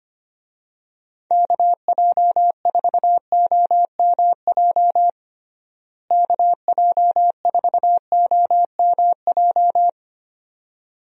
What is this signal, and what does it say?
Morse code, heard as a keyed tone, about 25 wpm, 700 Hz. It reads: KJ4OMJ KJ4OMJ
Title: USA Call Signs - 6 Characters - Repeated 2x - 25wpm